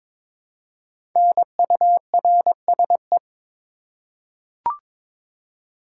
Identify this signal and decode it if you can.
Morse code, heard as a keyed tone, about 22 wpm, 700 Hz. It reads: NURSE
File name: 5 Letter Words - First 3 Letters Given - 22wpm